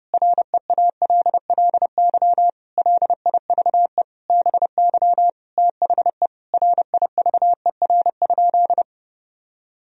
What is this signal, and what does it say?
Morse code, heard as a keyed tone, about 30 wpm, 700 Hz. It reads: REALLY LIVE BY THE RIVER?
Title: Sentences from Top 1000 Words - Review Part 1 - 30wpm